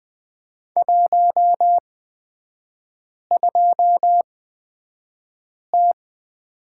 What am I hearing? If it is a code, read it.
Morse code, heard as a keyed tone, about 20 wpm, spaced as for 5 wpm, 690 Hz. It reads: 12T
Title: 229 Lesson 37e - Letter x - Three Characters 20wpm